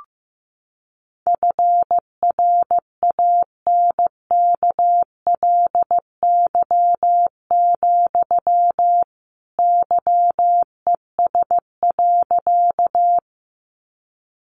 Morse code, heard as a keyed tone, about 15 wpm, 700 Hz. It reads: FRANKLY, YES.
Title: A Sound Of Thunder 15wpm